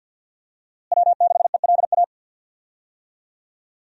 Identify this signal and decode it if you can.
Morse code, heard as a keyed tone, about 52 wpm, 700 Hz. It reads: W6ELA
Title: USA Call Signs - 5 Characters - Repeated 3x - 50wpm